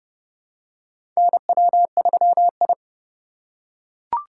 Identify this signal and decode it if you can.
Morse code, heard as a keyed tone, about 30 wpm, 700 Hz. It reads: NW3I
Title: USA Call Signs - 4 Characters - 2x Character Spacing - 30wpm